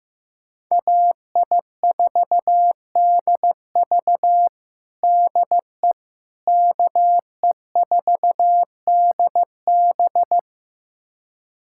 Morse code, heard as a keyed tone, about 15 wpm, 700 Hz. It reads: AI4DV DE KE4DB
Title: USA Callsign DE Callsign - 5 Characters - 15wpm